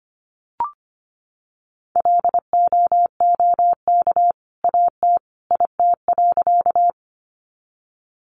Morse code, heard as a keyed tone, about 25 wpm, 700 Hz. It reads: LOOK AT IT.
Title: Sentences from Top 100 Words 25wpm